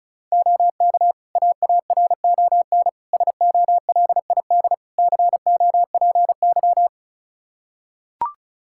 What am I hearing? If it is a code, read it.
Morse code, heard as a keyed tone, about 36 wpm, 700 Hz. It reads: OK AARON SOLID COPY
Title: K1USN SST QSO Practice 35wpm